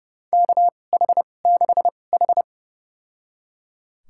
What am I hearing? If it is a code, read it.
Morse code, heard as a keyed tone, about 30 wpm, spaced as for 22 wpm, 700 Hz. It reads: KH6H
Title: USA Call Signs - 2x Character Spacing - 30wpm